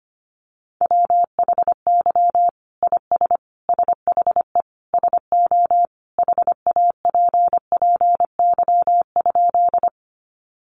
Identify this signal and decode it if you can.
Morse code, heard as a keyed tone, about 25 wpm, 700 Hz. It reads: WHY IS SHE SO HAPPY?
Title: Sentences from Top 1000 Words 25wpm